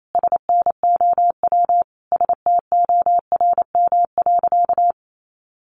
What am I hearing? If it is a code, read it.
Morse code, heard as a keyed tone, about 28 wpm, 700 Hz. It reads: SNOW STORM.
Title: Sentences from Top 1500 Words 28wpm